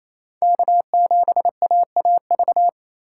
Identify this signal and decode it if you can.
Morse code, heard as a keyed tone, about 29 wpm, 700 Hz. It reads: K7AAV
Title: Speed Racing - USA 1x3 Call Signs 28wpm